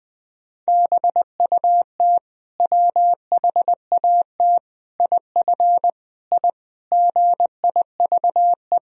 Morse code, heard as a keyed tone, about 20 wpm, 700 Hz. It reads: BUT WHAT IF I GIVE
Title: Sentences from Top 1000 Words - Review Part 2 - 20wpm